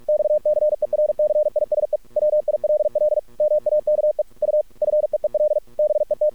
Morse code, heard as a keyed tone, about 50 wpm, 610 Hz. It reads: XCERCISE WILL MAKE U FEEL BET
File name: archived / CRSnet